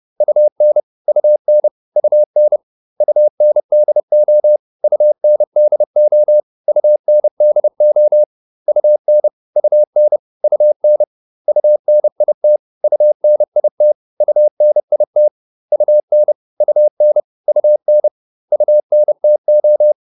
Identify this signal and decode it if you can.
Morse code, heard as a keyed tone, about 30 wpm, 600 Hz. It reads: UN UN UN UNDO UNDO UNDO UN UN UN UNIT UNIT UNIT UN UN UN UNTO